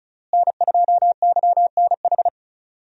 Morse code, heard as a keyed tone, about 36 wpm, 700 Hz. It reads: N2YNH
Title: USA Call Signs - 5 Characters - Repeated 3x - 35wpm